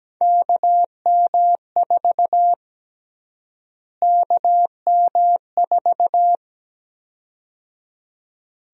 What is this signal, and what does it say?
Morse code, heard as a keyed tone, about 17 wpm, 700 Hz. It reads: KM4 KM4
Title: USA Call Sign Prefixes - Repeated 2x - 17wpm